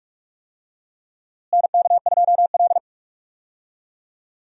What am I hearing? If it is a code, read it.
Morse code, heard as a keyed tone, about 42 wpm, 700 Hz. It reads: NK2L
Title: USA Call Signs - 4 Characters - Repeated 3x - 45wpm